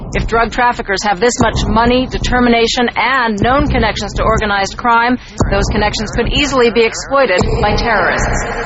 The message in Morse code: none

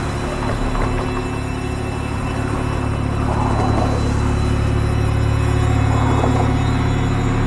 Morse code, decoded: none

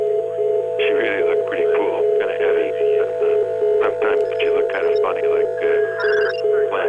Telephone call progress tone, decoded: congestion tone

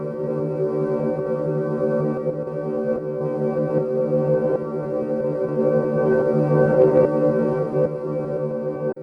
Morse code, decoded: none

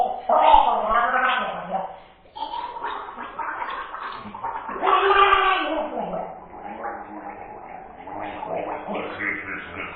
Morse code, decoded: none

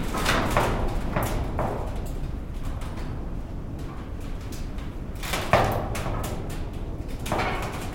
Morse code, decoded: none

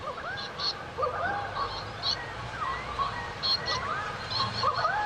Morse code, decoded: none

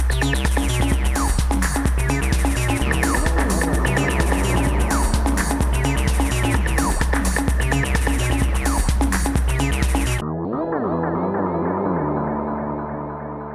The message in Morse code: none

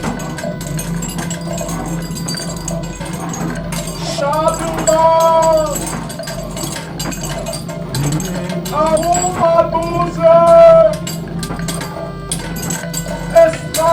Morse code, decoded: none

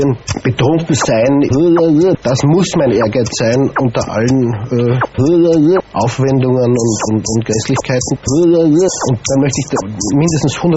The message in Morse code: none

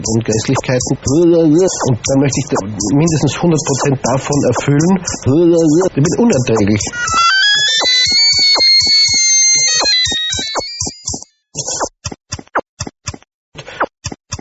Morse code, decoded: none

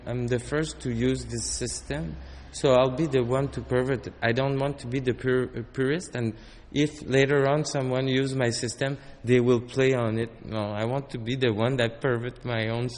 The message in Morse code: none

